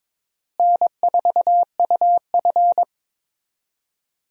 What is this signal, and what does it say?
Morse code, heard as a keyed tone, about 22 wpm, 700 Hz. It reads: N4UF
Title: USA Call Signs - 4 Characters - Repeated 2x - 22wpm